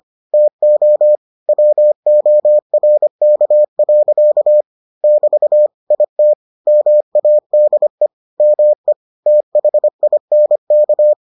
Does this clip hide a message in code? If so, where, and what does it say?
Yes: Morse code from 0.3 to 11.1 s, heard as a keyed tone, about 25 wpm, 600 Hz. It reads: TO WORK. <BT> IT MADE ME THINK